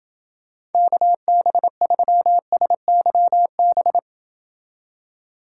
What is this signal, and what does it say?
Morse code, heard as a keyed tone, about 29 wpm, 700 Hz. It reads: KB3SYB